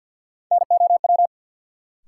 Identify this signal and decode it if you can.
Morse code, heard as a keyed tone, about 53 wpm, 700 Hz. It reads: NOW
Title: Sentences from Top 100 Words - New Word Drill - 50wpm